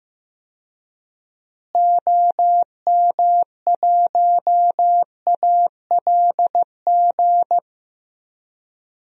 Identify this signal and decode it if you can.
Morse code, heard as a keyed tone, about 15 wpm, 700 Hz. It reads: OM1ALG
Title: DXCC-100 Call Signs - 6 Characters - Repeated 2x - 15wpm